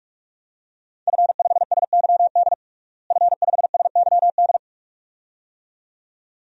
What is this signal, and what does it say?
Morse code, heard as a keyed tone, about 44 wpm, 700 Hz. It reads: F5SYD F5SYD
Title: DXCC-100 Call Signs - 5 Characters - Repeated 2x - 45wpm